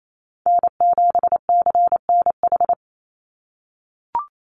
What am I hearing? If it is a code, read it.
Morse code, heard as a keyed tone, about 28 wpm, 700 Hz. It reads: N7CNH
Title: NA SOTA Chaser Callsigns 28wpm